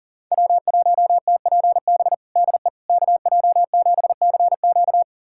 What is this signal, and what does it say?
Morse code, heard as a keyed tone, about 37 wpm, 700 Hz. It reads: W1TPB DE KJ7CQ
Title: USA Callsign DE Callsign - 5 Characters - 40wpm